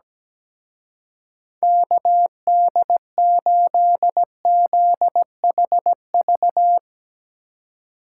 Morse code, heard as a keyed tone, about 17 wpm, 700 Hz. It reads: KD8ZHV
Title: Speed Racing - USA 2x3 Call Signs 17wpm